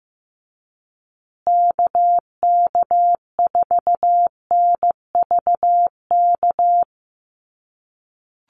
Morse code, heard as a keyed tone, about 15 wpm, 700 Hz. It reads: KK4NVK
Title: USA Call Signs Repeated 2x 15wpm